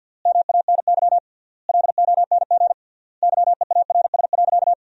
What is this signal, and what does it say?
Morse code, heard as a keyed tone, about 52 wpm, 700 Hz. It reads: MANY LONG YEARS.
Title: 3x Word-Spacing - Sentences from Top 200 Words - 50wpm